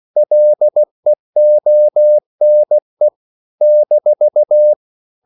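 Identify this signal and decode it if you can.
Morse code, heard as a keyed tone, about 16 wpm, 600 Hz. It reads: LEONE -
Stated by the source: CW practice of national capitals at 16 words per minute, from ON6ZQ.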